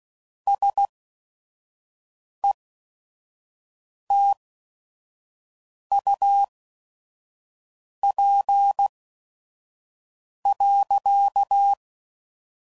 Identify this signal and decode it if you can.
Morse code, heard as a keyed tone, about 16 wpm, spaced as for 5 wpm, 800 Hz. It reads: SETUP.
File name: CW Code Groups @5WPM / QSOs / WD0FBT 1